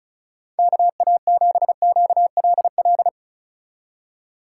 Morse code, heard as a keyed tone, about 33 wpm, 700 Hz. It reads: KA7QLL
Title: Speed Racing - USA 2x3 Call Signs 35wpm